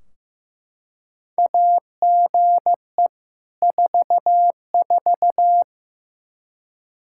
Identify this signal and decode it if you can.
Morse code, heard as a keyed tone, about 15 wpm, 700 Hz. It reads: AGE 44
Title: Practice QSO - Exchange 3 - More Info - 15wpm